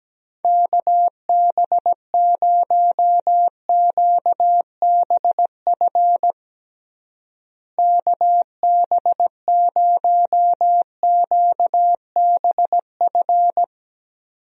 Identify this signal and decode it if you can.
Morse code, heard as a keyed tone, about 17 wpm, 700 Hz. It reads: KB0QBF KB0QBF
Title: USA Call Signs - 6 Characters - Repeated 2x - 17wpm